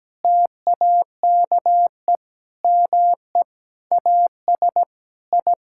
Morse code, heard as a keyed tone, about 17 wpm, 700 Hz. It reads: TAKE ME AS I
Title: Sentences from Top 1000 Words - Review Part 6 - 17wpm